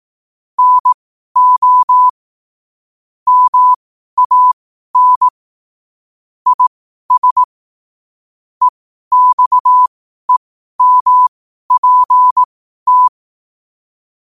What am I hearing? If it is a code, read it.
Morse code, heard as a keyed tone, about 18 wpm, spaced as for 12 wpm, 1000 Hz. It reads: NO MAN IS EXEMPT
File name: qotdfiles / 12wpm